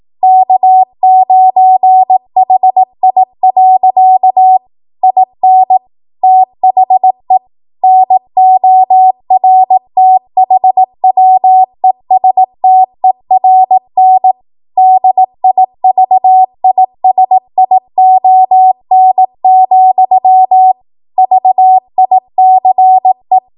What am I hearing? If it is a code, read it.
Morse code, heard as a keyed tone, about 18 wpm, 750 Hz. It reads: K9HI. IN THE NORTHWESTERN DIVISION, VICE